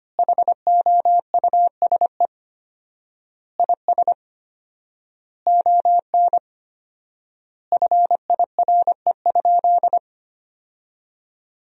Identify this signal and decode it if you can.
Morse code, heard as a keyed tone, about 25 wpm, 700 Hz. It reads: HOUSE IS ON FIRE?
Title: 4x Word-Spacing - Sentences from Top 400 Words - 25wpm